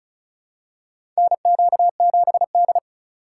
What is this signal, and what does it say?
Morse code, heard as a keyed tone, about 34 wpm, 700 Hz. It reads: NQ7D
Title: USA Call Signs - 4 Characters - 6x Character Spacing - 35wpm